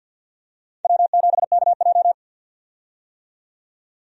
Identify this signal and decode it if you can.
Morse code, heard as a keyed tone, about 52 wpm, 700 Hz. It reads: W7KJ
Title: USA Call Signs - 4 Characters - 50wpm